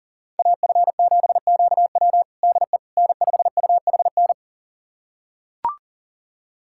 Morse code, heard as a keyed tone, about 40 wpm, 700 Hz. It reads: AF7QW DE N5UHN
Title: USA Callsign DE Callsign - 5 Characters - 40wpm